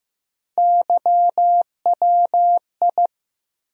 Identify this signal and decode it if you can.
Morse code, heard as a keyed tone, about 15 wpm, 700 Hz. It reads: YWI
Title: USA Call Sign Postfixes - 3 Letters 15wpm